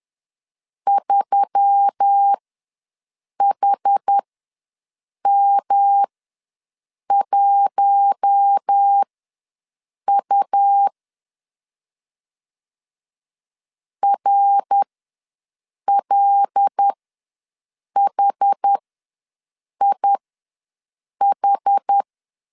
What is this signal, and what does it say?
Morse code, heard as a keyed tone, about 10 wpm, spaced as for 6 wpm, 800 Hz. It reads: 3HM1U RLHIH
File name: LEZIONI / 5 LBVC / 5 L ws 10wpm 3636